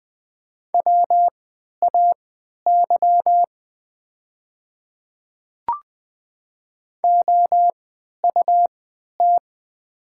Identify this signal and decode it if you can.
Morse code, heard as a keyed tone, about 20 wpm, spaced as for 11 wpm, 690 Hz. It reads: WAY OUT
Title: 3x Character Spacing - Top 200 Words - 20wpm